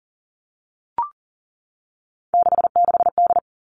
Morse code, heard as a keyed tone, about 40 wpm, 700 Hz. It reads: 66D
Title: Three Challenging Letter-Number Combinations 40wpm